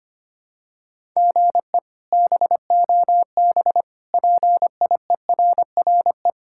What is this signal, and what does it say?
Morse code, heard as a keyed tone, about 25 wpm, 700 Hz. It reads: GE BOB PIERRE
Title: K1USN SST QSO Practice - 25wpm